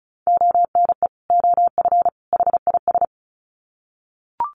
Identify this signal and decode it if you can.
Morse code, heard as a keyed tone, about 34 wpm, 700 Hz. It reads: ONE OF HIS
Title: Sets of 3 Words 35wpm